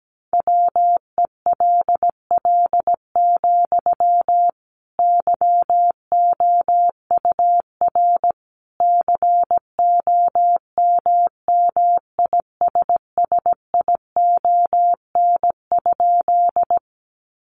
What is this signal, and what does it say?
Morse code, heard as a keyed tone, about 17 wpm, 700 Hz. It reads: WELL, YOUR COMMISSION?
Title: Sentences from Top 1400 Words 17wpm